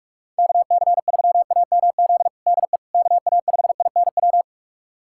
Morse code, heard as a keyed tone, about 46 wpm, 700 Hz. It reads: KC3AMZ DE KA5INW
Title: USA Callsign DE Callsign - 6 Characters - 45wpm